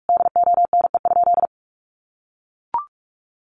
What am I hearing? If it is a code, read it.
Morse code, heard as a keyed tone, about 44 wpm, 700 Hz. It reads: DONE?
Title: Sentences from Top 1100 Words 45wpm